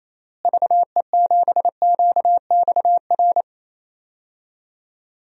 Morse code, heard as a keyed tone, about 28 wpm, 700 Hz. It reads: VE7QXR
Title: DXCC-100 Call Signs - 6 Characters - Repeated 2x - 28wpm